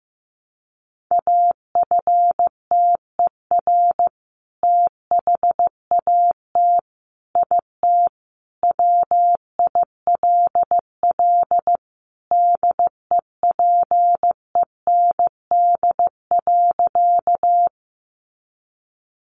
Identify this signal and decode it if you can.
Morse code, heard as a keyed tone, about 14 wpm, 700 Hz. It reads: AFTER THAT IT WILL DEPEND.
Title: Sentences from Top 1300 Words 15wpm